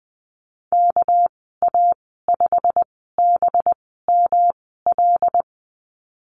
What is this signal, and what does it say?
Morse code, heard as a keyed tone, about 20 wpm, spaced as for 15 wpm, 700 Hz. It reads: KA5BML
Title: 200 Lesson 32g - Letter k - Call Signs 20wpm